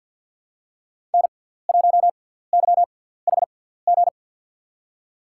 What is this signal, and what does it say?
Morse code, heard as a keyed tone, about 52 wpm, spaced as for 17 wpm, 700 Hz. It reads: N1YHG
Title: USA Call Signs - 5 Characters - 6x Character Spacing - 50wpm